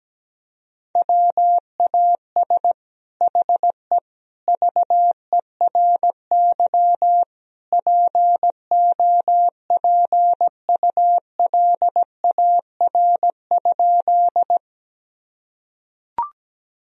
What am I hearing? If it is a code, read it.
Morse code, heard as a keyed tone, about 17 wpm, 700 Hz. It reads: WAS HE VERY POPULAR?
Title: Sentences from Top 1000 Words - Review Part 3 - 17wpm